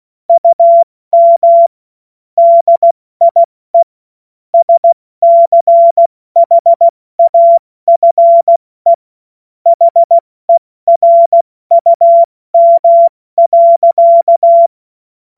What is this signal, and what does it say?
Morse code, heard as a keyed tone, about 16 wpm, 680 Hz. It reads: UM DIE SCHAFE HERUM.
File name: Morsekurs / Witze / Witz 26 Dezember 2023 70 F80